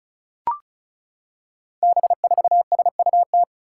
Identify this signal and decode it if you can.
Morse code, heard as a keyed tone, about 36 wpm, 700 Hz. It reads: B4SUT
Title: Five Challenging Letter-Number Combinations 35wpm